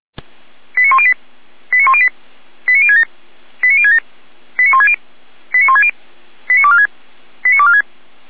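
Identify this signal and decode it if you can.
Five-tone selective call (ZVEI), heard as quick runs of five tones, tones of about 70 ms. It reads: 89108 89108 89067 89067 89160 89160 89246 89246
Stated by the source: ZVEI five-tone selective call 89560 received on SDR.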